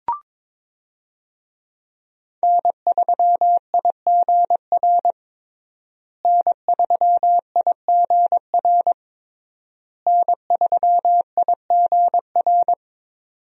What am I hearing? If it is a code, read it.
Morse code, heard as a keyed tone, about 22 wpm, 700 Hz. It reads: N3IGR N3IGR N3IGR
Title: USA Call Signs - 5 Characters - Repeated 3x - 22wpm